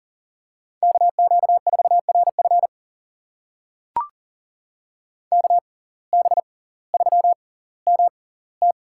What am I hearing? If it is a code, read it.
Morse code, heard as a keyed tone, about 40 wpm, spaced as for 21 wpm, 690 Hz. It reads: KQ4RF KB3MT
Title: USA Call Signs - 5 Characters - 6x Character Spacing - 40wpm